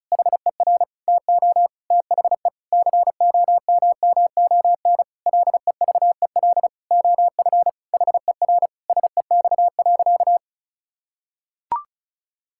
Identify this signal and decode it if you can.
Morse code, heard as a keyed tone, about 34 wpm, 700 Hz. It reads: HER TO THE COMMON LEVEL OF HER SEX.
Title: Sentences from Top 1600 Words 35wpm